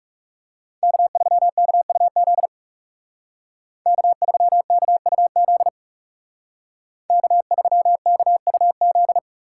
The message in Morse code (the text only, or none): K3KUZ K3KUZ K3KUZ